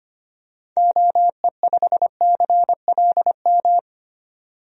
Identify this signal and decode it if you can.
Morse code, heard as a keyed tone, about 25 wpm, 690 Hz. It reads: OE5CLM
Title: DXCC-100 Call Signs - 6 Characters - 25wpm